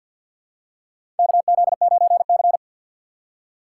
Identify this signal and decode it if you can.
Morse code, heard as a keyed tone, about 51 wpm, 690 Hz. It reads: KZ9C
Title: USA Call Signs - 4 Characters - 6x Character Spacing - 50wpm